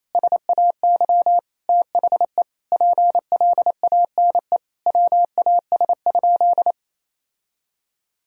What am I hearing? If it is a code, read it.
Morse code, heard as a keyed tone, about 28 wpm, 700 Hz. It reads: SAY THE PLANE WAS?
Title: Sentences from Top 1000 Words - Review Part 3 - 28wpm